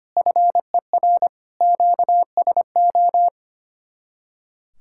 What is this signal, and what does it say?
Morse code, heard as a keyed tone, about 31 wpm, 700 Hz. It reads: FER QSO
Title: Speed-Racing Common QSO Phrases Encore - 25wpm